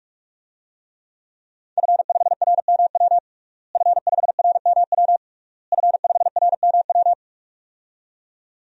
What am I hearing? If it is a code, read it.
Morse code, heard as a keyed tone, about 46 wpm, 700 Hz. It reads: F5RMW F5RMW F5RMW
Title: DXCC-100 Call Signs - 5 Characters - Repeated 3x - 45wpm